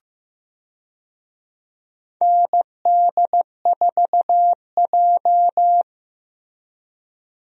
Morse code, heard as a keyed tone, about 15 wpm, 700 Hz. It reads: ND4J
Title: USA Call Signs - 4 Characters - Repeated 3x - 15wpm